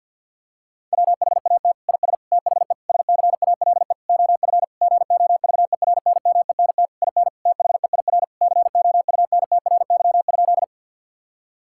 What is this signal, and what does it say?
Morse code, heard as a keyed tone, about 50 wpm, 700 Hz. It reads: WHAT IS THE SCALE OF GOVERNMENT IN THEIR COUNTRY?
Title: Sentences from Top 1000 Words - Review Part 5 - 50wpm